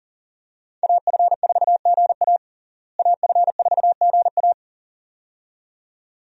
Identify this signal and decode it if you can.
Morse code, heard as a keyed tone, about 40 wpm, 700 Hz. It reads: AF4GA AF4GA
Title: USA Call Signs - 5 Characters - Repeated 2x - 40wpm